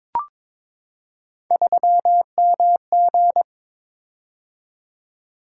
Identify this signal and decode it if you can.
Morse code, heard as a keyed tone, about 22 wpm, 700 Hz. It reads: 3MG
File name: practice / 22wpm / Three Letter-Number Combinations - 22wpm